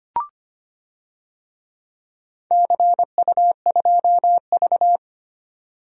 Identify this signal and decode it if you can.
Morse code, heard as a keyed tone, about 25 wpm, 700 Hz. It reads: CU2V
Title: DXCC-100 Call Signs - 4 Characters - 25wpm